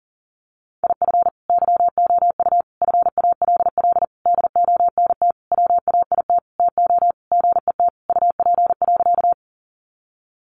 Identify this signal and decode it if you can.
Morse code, heard as a keyed tone, about 40 wpm, 700 Hz. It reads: IF YOU FALL DONT WAIT TO GET UP.
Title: Sentences from Top 500 Words 40wpm